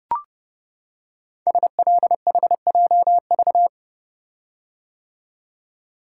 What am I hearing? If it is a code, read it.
Morse code, heard as a keyed tone, about 30 wpm, 700 Hz. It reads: SLHJV
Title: Code Groups - 5 Letters - 30wpm